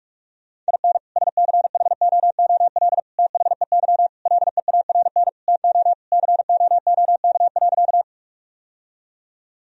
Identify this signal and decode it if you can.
Morse code, heard as a keyed tone, about 43 wpm, 700 Hz. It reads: IN SCHOOL THEY LEARN TO COOK.
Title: Sentences from Top 1000 Words - Review Part 4 - 45wpm